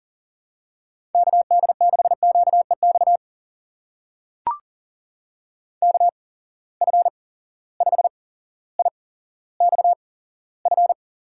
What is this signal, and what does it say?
Morse code, heard as a keyed tone, about 39 wpm, spaced as for 17 wpm, 700 Hz. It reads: KD6QEX KF5IXF
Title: USA Call Signs - 6 Characters - 8x Character Spacing - 40wpm